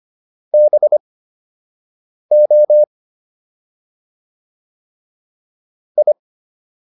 Morse code, heard as a keyed tone, about 25 wpm, spaced as for 6 wpm, 600 Hz. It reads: BO I